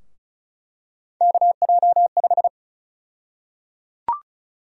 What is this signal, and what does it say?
Morse code, heard as a keyed tone, about 34 wpm, 700 Hz. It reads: KJ5
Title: USA Call Sign Prefixes - 35wpm